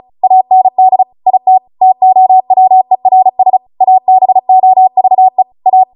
Morse code, heard as a keyed tone, about 36 wpm, 750 Hz. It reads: AND IT TOWERS ABOVE A
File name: Morse / 35WPM